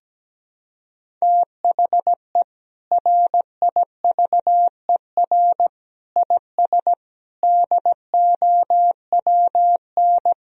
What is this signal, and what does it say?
Morse code, heard as a keyed tone, about 17 wpm, 700 Hz. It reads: THE RIVER IS DOWN
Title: Sentences from Top 1000 Words - Review Part 1 - 17wpm